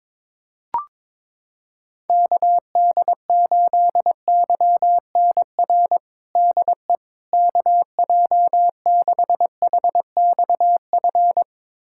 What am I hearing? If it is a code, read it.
Morse code, heard as a keyed tone, about 22 wpm, 700 Hz. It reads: KD8YNR DE KJ6HXF
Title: USA Callsign DE Callsign 22wpm